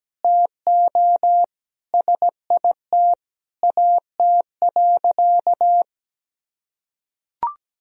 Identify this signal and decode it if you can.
Morse code, heard as a keyed tone, about 17 wpm, 700 Hz. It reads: TO SIT AT.
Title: Sentences from Top 1000 Words - Review Part 2 - 17wpm